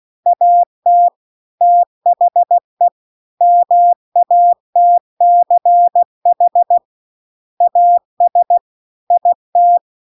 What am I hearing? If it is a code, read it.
Morse code, heard as a keyed tone, about 16 wpm, 700 Hz. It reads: AT THE MATCH AS IT
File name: CW-The-Big-Twoheaded-River-2-16WPM